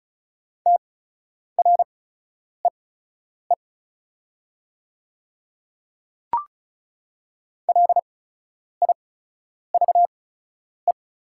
Morse code, heard as a keyed tone, about 34 wpm, spaced as for 10 wpm, 700 Hz. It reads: TREE LIVE
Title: 8x Character Spacing - Top 300 Words - 35wpm